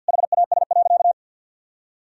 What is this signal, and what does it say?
Morse code, heard as a keyed tone, about 52 wpm, 700 Hz. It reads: HAS.
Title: Sentences from Top 500 Words and Abbreviations 50wpm